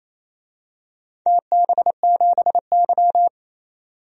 Morse code, heard as a keyed tone, about 28 wpm, 700 Hz. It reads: TB7Y
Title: DXCC-100 Call Signs - 4 Characters - 28wpm